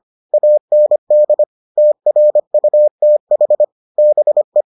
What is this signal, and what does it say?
Morse code, heard as a keyed tone, about 25 wpm, 600 Hz. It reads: AND TRUTH BE